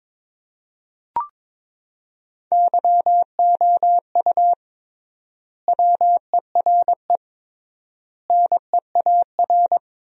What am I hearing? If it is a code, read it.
Morse code, heard as a keyed tone, about 22 wpm, 700 Hz. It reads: YOU WERE NEAR
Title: 3x Word-Spacing - Sentences from Top 200 Words - 22wpm